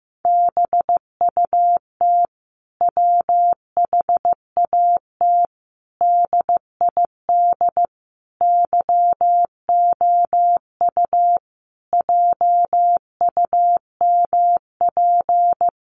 Morse code, heard as a keyed tone, about 15 wpm, 700 Hz. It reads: BUT WHAT DID YOU JUMP
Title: Sentences from Top 1300 Words 15wpm